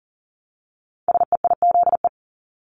Morse code, heard as a keyed tone, about 40 wpm, 690 Hz. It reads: SEIZE
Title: Multi-letter Phonemes Repeated 3x 40wpm